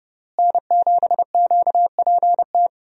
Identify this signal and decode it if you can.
Morse code, heard as a keyed tone, about 30 wpm, 700 Hz. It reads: N7QPT
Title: USA Call Signs - 5 Characters - Repeated 3x - 30wpm